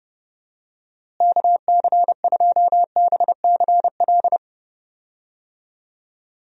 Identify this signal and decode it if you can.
Morse code, heard as a keyed tone, about 32 wpm, 700 Hz. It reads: KC2BCL
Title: USA Call Signs - 6 Characters - Repeated 2x - 30wpm